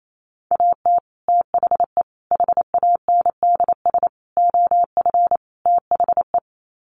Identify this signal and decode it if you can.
Morse code, heard as a keyed tone, about 28 wpm, 700 Hz. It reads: AT THE HANDS OF THE
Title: Sets of 5 Words Encore2 28wpm